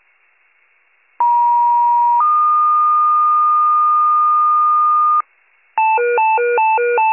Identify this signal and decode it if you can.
Two-tone page, heard as a one-second tone, then a three-second tone, about 950 then 1220 Hz, with noise. It is heard from 1.2 to 5.2 s.
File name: pager / 94 RESCUE OFC / wcalert